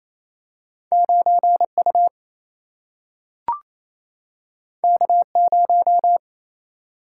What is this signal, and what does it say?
Morse code, heard as a keyed tone, about 28 wpm, 700 Hz. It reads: 9U K0
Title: Single-Letter Single-Number 28wpm